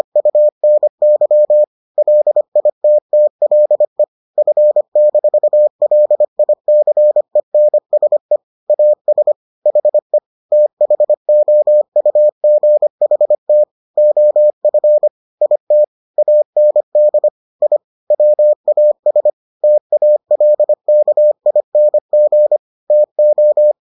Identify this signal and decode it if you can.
Morse code, heard as a keyed tone, about 25 wpm, 600 Hz. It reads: UNY LITTLE F-LICENSE AS HE THOUGHT OF IT AND I WAS TALKING TO